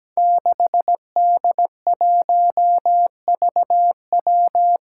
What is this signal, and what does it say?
Morse code, heard as a keyed tone, about 17 wpm, 700 Hz. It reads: 6D1VW
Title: Five Challenging Letter-Number Combinations 17wpm